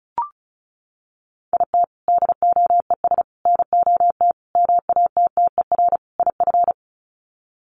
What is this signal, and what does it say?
Morse code, heard as a keyed tone, about 34 wpm, 700 Hz. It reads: IT DOES NOT MATTER IF
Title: Sets of 5 Words Encore 35wpm